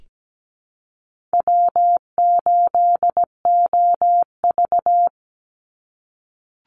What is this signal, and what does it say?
Morse code, heard as a keyed tone, about 17 wpm, 700 Hz. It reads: W8OV
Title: Easy Call Signs - 17wpm